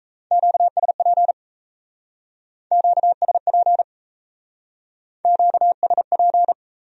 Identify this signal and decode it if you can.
Morse code, heard as a keyed tone, about 38 wpm, 700 Hz. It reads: QSP QSP QSP